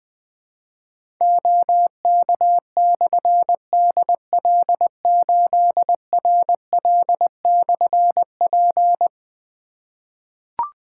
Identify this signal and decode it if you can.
Morse code, heard as a keyed tone, about 20 wpm, 700 Hz. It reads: OK/DL8RL/P
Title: Difficult Call Signs 20wpm